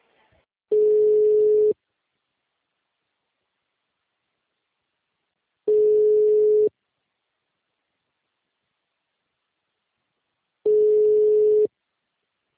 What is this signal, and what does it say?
Ringback tone (Europe, 425 Hz).